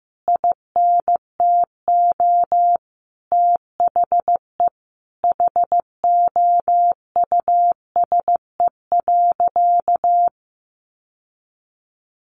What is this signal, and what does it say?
Morse code, heard as a keyed tone, about 15 wpm, 700 Hz. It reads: INTO THE HOUSE.
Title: Sentences from Top 600 Words 15wpm